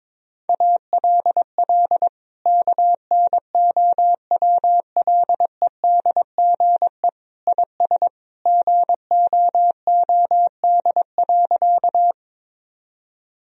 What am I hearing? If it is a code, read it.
Morse code, heard as a keyed tone, about 22 wpm, 700 Hz. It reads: ALL KNOWLEDGE IS GOOD.
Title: Sentences from Top 1000 Words - Review Part 3 - 22wpm